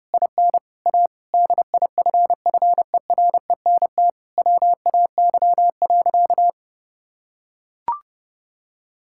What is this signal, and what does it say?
Morse code, heard as a keyed tone, about 30 wpm, 700 Hz. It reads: IN A DIFFERENT WAY.
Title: Sentences from Top 1000 Words - Review Part 6 - 30wpm